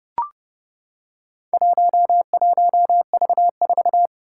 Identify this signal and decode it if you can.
Morse code, heard as a keyed tone, about 30 wpm, 700 Hz. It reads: 11V4
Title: Four Challenging Letter-Number Combinations 30wpm